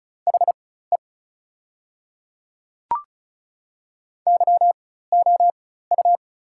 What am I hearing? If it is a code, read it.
Morse code, heard as a keyed tone, about 36 wpm, spaced as for 16 wpm, 700 Hz. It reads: HE YOU
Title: 4x Character Spacing - Top 200 Words - 35wpm